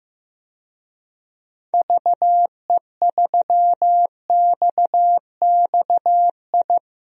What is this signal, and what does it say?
Morse code, heard as a keyed tone, about 15 wpm, 700 Hz. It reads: VE3XXI
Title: DXCC-100 Call Signs - 6 Characters - 15wpm